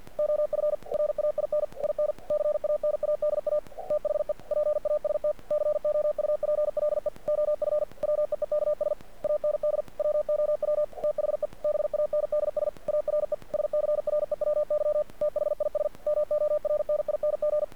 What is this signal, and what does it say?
Morse code, heard as a keyed tone, about 48 wpm, 600 Hz. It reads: OF RAIN IN CANADA THE PAST COUPLE OF WEEKS AND WOW THE BANDS ARE SCREWY THIS MORNING
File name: archived / CRSnet